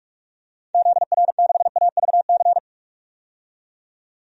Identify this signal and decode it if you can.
Morse code, heard as a keyed tone, about 41 wpm, 700 Hz. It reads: ZR6AVC
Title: DXCC-100 Call Signs - 6 Characters - 45wpm